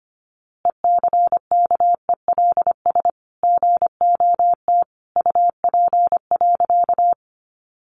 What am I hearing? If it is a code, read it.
Morse code, heard as a keyed tone, about 25 wpm, 700 Hz. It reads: ECKELS GOT UP.
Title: A Sound Of Thunder 25wpm